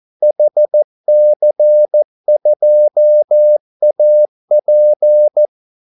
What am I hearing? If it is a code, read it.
Morse code, heard as a keyed tone, about 14 wpm, 600 Hz. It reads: HC2AP
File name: Callsigns-14wpm